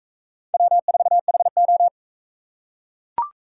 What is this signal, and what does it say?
Morse code, heard as a keyed tone, about 41 wpm, 700 Hz. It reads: W4HO